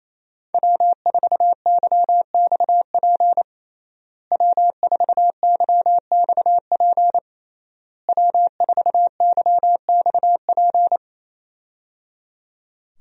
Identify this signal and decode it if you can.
Morse code, heard as a keyed tone, about 28 wpm, 700 Hz. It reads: W4YXP W4YXP W4YXP
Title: USA Call Signs - 5 Characters - Repeated 3x - 28wpm